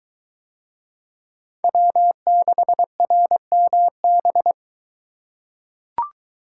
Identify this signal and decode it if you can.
Morse code, heard as a keyed tone, about 21 wpm, 700 Hz. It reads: W6RMB